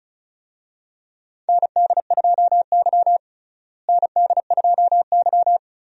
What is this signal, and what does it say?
Morse code, heard as a keyed tone, about 34 wpm, 700 Hz. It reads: ND2Y ND2Y
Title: USA Call Signs - 4 Characters - Repeated 2x - 35wpm